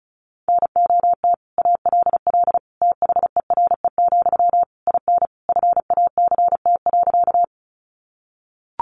Morse code, heard as a keyed tone, about 34 wpm, 700 Hz. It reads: NOT ALL THERE, IN FACT.
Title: Sentences from Top 600 Words 35wpm